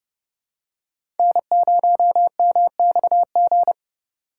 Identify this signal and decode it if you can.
Morse code, heard as a keyed tone, about 29 wpm, 700 Hz. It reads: N0MXG
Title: USA Call Signs - 5 Characters - 4x Character Spacing - 30wpm